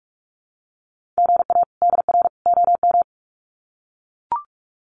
Morse code, heard as a keyed tone, about 44 wpm, 700 Hz. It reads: GA DR OM
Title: Common QSO Phrases 45wpm